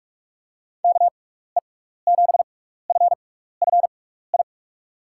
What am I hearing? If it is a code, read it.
Morse code, heard as a keyed tone, about 44 wpm, spaced as for 15 wpm, 700 Hz. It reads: KE7FFI
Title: USA Call Signs - 6 Characters - 6x Character Spacing - 45wpm